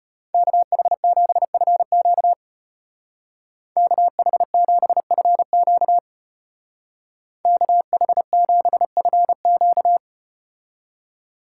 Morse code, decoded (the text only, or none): KH7FQ KH7FQ KH7FQ